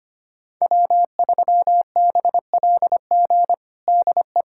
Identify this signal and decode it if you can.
Morse code, heard as a keyed tone, about 25 wpm, 700 Hz. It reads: W3BLG DE
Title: USA Callsign DE Callsign - 5 Characters - 25wpm